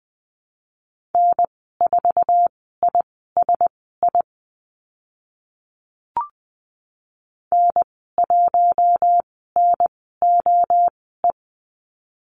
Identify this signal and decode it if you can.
Morse code, heard as a keyed tone, about 20 wpm, spaced as for 15 wpm, 700 Hz. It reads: N4ISI N1NOE
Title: 055 Lesson 10g - Letter r - Call Signs 20wpm